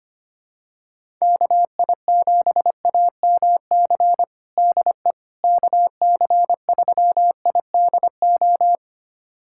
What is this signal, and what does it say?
Morse code, heard as a keyed tone, about 25 wpm, 700 Hz. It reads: KI7AMC DE KC3IDO
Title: USA Callsign DE Callsign 25wpm